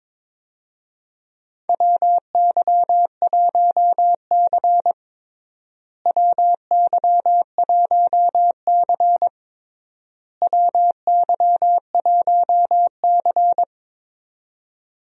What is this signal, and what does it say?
Morse code, heard as a keyed tone, about 22 wpm, 700 Hz. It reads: WY1C WY1C WY1C
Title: USA Call Signs - 4 Characters - Repeated 3x - 22wpm